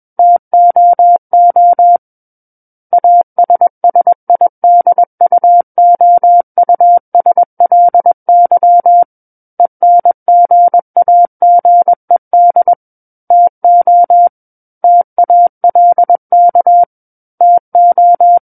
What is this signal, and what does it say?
Morse code, heard as a keyed tone, about 21 wpm, 700 Hz. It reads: TOO ASSIDUOUSLY ENGAGED TO TALK TO
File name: Book One / 12-Chapter 12 - 21 WPM